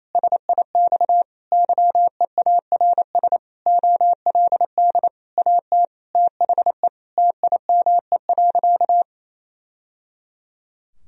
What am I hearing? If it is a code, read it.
Morse code, heard as a keyed tone, about 28 wpm, 700 Hz. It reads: SIX YEARS OLD AT THE TIME.
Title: Sentences from Top 1000 Words - Review Part 2 - 28wpm